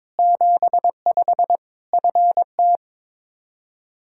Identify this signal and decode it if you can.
Morse code, heard as a keyed tone, about 22 wpm, 700 Hz. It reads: 75 FT
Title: Practice QSO - Exchange 2 - Info - 22wpm